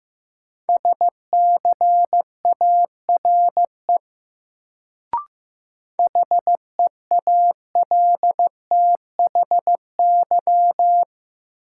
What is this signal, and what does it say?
Morse code, heard as a keyed tone, about 15 wpm, 700 Hz. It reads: SCARE HEALTHY
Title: Sentences from Top 1000 Words - Review Part 5 - New Word Drill - 15wpm